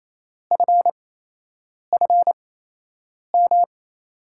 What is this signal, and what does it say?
Morse code, heard as a keyed tone, about 28 wpm, spaced as for 8 wpm, 700 Hz. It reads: FFM
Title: USA Call Signs - 6 Characters - 8x Character Spacing - 28wpm